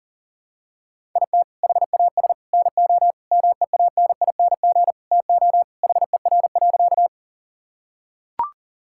Morse code, heard as a keyed tone, about 37 wpm, 700 Hz. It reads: IT HAS NO MEANING TO HER.
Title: Sentences from Top 1000 Words - Review Part 6 - 40wpm